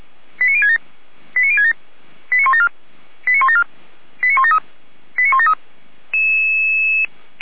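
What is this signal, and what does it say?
Five-tone selective call (ZVEI), heard as quick runs of five tones, tones of about 70 ms. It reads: 89067 89067 89174 89174 89173 89173
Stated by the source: ZVEI five-tone selective call 89246 received with an SDR.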